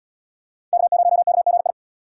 Morse code, heard as a keyed tone, about 77 wpm, 700 Hz. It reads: K0KGI